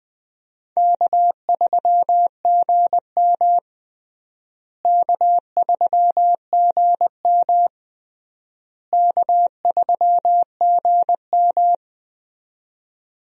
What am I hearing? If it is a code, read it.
Morse code, heard as a keyed tone, about 20 wpm, 700 Hz. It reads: K3GM K3GM K3GM
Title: USA Call Signs - 4 Characters - Repeated 3x - 20wpm